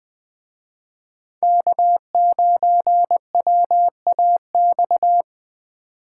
Morse code, heard as a keyed tone, about 24 wpm, 700 Hz. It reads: K9WAX
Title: Speed Racing - USA 1x3 Call Signs 20wpm